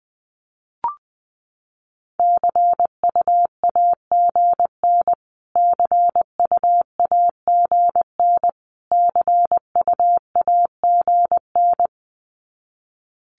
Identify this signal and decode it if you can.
Morse code, heard as a keyed tone, about 20 wpm, 700 Hz. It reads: CUAGN CUAGN CUAGN
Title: Abbreviations Repeated 3x - 20wpm